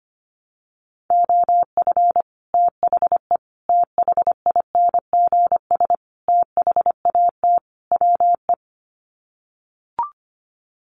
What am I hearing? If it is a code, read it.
Morse code, heard as a keyed tone, about 25 wpm, 700 Hz. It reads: OF THE THINGS THAT WE
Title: Sets of 5 Words Encore 25wpm